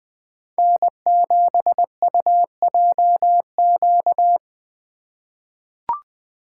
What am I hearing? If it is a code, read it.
Morse code, heard as a keyed tone, about 20 wpm, 700 Hz. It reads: N7UJQ
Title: USA Call Signs - 2x Character Spacing - 20wpm